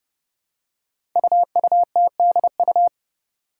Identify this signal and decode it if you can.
Morse code, heard as a keyed tone, about 30 wpm, 700 Hz. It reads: UUTDU
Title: Code Groups - 5 Letters - 30wpm